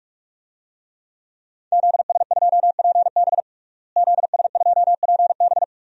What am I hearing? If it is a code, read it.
Morse code, heard as a keyed tone, about 46 wpm, 700 Hz. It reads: ZS2PB ZS2PB
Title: DXCC-100 Call Signs - 5 Characters - Repeated 2x - 45wpm